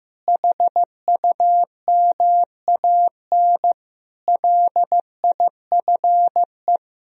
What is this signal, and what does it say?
Morse code, heard as a keyed tone, about 15 wpm, 700 Hz. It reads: HUMAN LIFE
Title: Sentences from Top 1000 Words - Review Part 3 - 15wpm